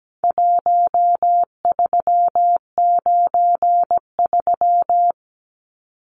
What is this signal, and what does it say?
Morse code, heard as a keyed tone, about 17 wpm, 700 Hz. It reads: 1393
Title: Numbers - 4-Digit 17wpm